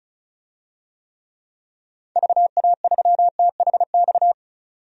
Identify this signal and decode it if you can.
Morse code, heard as a keyed tone, about 34 wpm, 700 Hz. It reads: VA3THX
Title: DXCC-100 Call Signs - 6 Characters - 35wpm